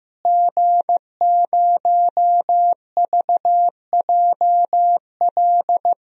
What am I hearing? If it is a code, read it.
Morse code, heard as a keyed tone, about 15 wpm, 700 Hz. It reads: G0VJL
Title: DXCC-100 Call Signs - 5 Characters - Repeated 3x - 15wpm